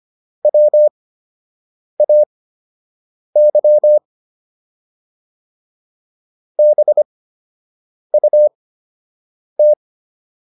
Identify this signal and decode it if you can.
Morse code, heard as a keyed tone, about 25 wpm, spaced as for 7 wpm, 600 Hz. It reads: WAY BUT